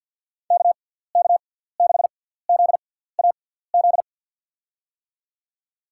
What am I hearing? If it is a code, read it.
Morse code, heard as a keyed tone, about 52 wpm, spaced as for 17 wpm, 700 Hz. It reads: KK6ZAZ
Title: USA Call Signs - 6x Character Spacing - 50wpm